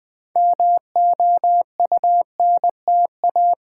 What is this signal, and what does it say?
Morse code, heard as a keyed tone, about 20 wpm, 700 Hz. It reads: MOUNTA
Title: Build A Word - 2 Letters At A Time - 8-Letter Words - 20wpm